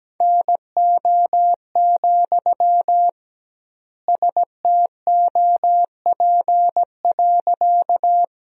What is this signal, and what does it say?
Morse code, heard as a keyed tone, about 17 wpm, 700 Hz. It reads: NO, STOP.
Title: 2x Word-Spacing - Sentences from Top 300 Words - 17wpm